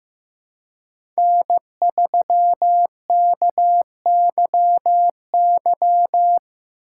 Morse code, heard as a keyed tone, about 15 wpm, 700 Hz. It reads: N3KYY
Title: USA Call Signs - 5 Characters - 3x Character Spacing - 15wpm